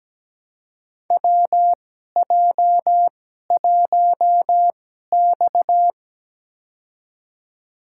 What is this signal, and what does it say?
Morse code, heard as a keyed tone, about 17 wpm, spaced as for 12 wpm, 700 Hz. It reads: WJ1X
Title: USA Call Signs - 2x Character Spacing - 17wpm